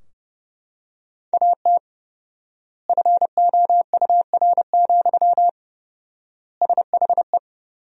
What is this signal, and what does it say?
Morse code, heard as a keyed tone, about 32 wpm, 700 Hz. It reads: AT FOUR, SHE
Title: 4x Word-Spacing - Sentences from Top 300 Words - 30wpm